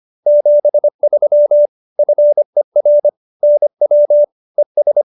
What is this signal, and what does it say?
Morse code, heard as a keyed tone, about 25 wpm, 590 Hz. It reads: 73 FER NW ES